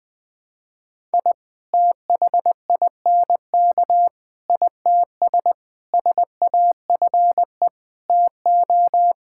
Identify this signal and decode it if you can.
Morse code, heard as a keyed tone, about 20 wpm, 700 Hz. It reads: I THINK ITS SAFE TO
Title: Sets of 6 Words - 20wpm